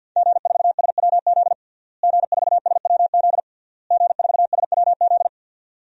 Morse code, heard as a keyed tone, about 52 wpm, 700 Hz. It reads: G4SWZ G4SWZ G4SWZ
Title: DXCC-100 Call Signs - 5 Characters - Repeated 3x - 50wpm